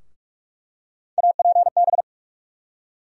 Morse code, heard as a keyed tone, about 46 wpm, 700 Hz. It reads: APB
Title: USA Call Sign Postfixes - 3 Letters 45wpm